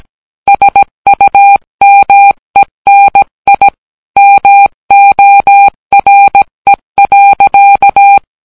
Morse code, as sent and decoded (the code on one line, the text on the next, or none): ... ..- -- . -. .. / -- --- .-. .-.-.-
SUMENI MORE.